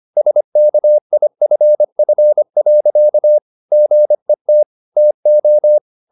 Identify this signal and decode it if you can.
Morse code, heard as a keyed tone, about 25 wpm, 600 Hz. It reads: SKIFF. GET TO